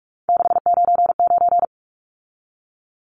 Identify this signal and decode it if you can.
Morse code, heard as a keyed tone, about 44 wpm, 700 Hz. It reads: 699
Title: Numbers - 3-Digit 45wpm